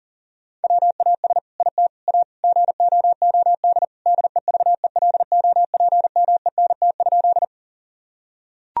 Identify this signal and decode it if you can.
Morse code, heard as a keyed tone, about 40 wpm, 700 Hz. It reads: WAS IT A GOOD DEVELOPMENT?
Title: Sentences from Top 1000 Words - Review Part 4 - 40wpm